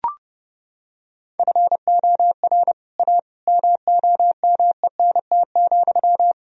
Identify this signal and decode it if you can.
Morse code, heard as a keyed tone, about 30 wpm, 700 Hz. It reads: FOR A MOMENT,
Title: Sentences from Top 1000 Words - Review Part 2 - 30wpm